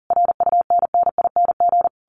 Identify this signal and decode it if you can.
Morse code, heard as a keyed tone, about 40 wpm, 700 Hz. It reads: RUNNING
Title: A Sound Of Thunder 40wpm